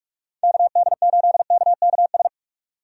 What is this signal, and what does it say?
Morse code, heard as a keyed tone, about 46 wpm, 700 Hz. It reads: KD8KKS
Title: USA Call Signs - 6 Characters - 45wpm